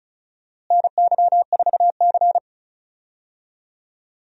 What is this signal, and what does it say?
Morse code, heard as a keyed tone, about 36 wpm, 700 Hz. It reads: NY4C
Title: USA Call Signs - 4 Characters - Repeated 3x - 35wpm